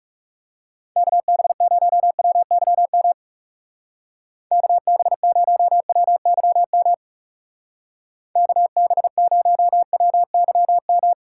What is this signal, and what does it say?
Morse code, heard as a keyed tone, about 37 wpm, 700 Hz. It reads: KB0WYM KB0WYM KB0WYM
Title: Speed Racing - USA 2x3 Call Signs 30wpm